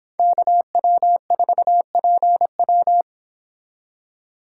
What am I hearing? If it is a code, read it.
Morse code, heard as a keyed tone, about 26 wpm, 700 Hz. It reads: KW4PW